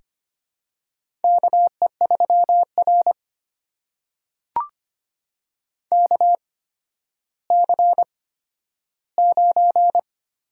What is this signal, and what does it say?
Morse code, heard as a keyed tone, about 25 wpm, spaced as for 12 wpm, 700 Hz. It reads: KE3R KC9
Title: USA Call Signs - 8x Character Spacing - 25wpm